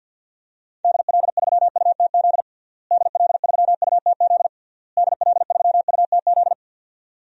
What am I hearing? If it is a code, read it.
Morse code, heard as a keyed tone, about 51 wpm, 700 Hz. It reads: DL3UTZ DL3UTZ DL3UTZ
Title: DXCC-100 Call Signs - 6 Characters - Repeated 3x - 50wpm